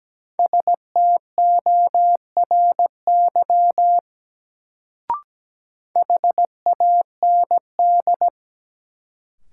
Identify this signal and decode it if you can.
Morse code, heard as a keyed tone, about 17 wpm, 700 Hz. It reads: STORY HAND
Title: Sets of 3 Words - New Word Drill - 17wpm